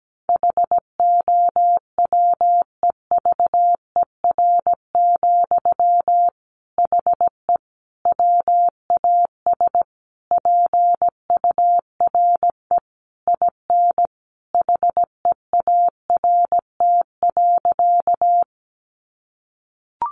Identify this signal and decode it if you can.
Morse code, heard as a keyed tone, about 17 wpm, 700 Hz. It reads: HOWEVER, HE WAS PURE IN HEART.
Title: Sentences from Top 1500 Words 17wpm